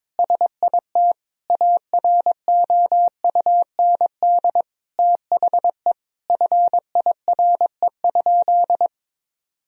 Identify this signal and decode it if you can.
Morse code, heard as a keyed tone, about 21 wpm, 700 Hz. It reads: SIT AROUND THE FIRE?
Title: Sentences from Top 1000 Words - Review Part 3 - 22wpm